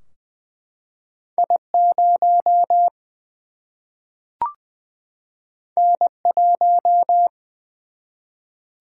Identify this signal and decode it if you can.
Morse code, heard as a keyed tone, about 20 wpm, 700 Hz. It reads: I 0 N 1
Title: Single-Letter Single-Number 20wpm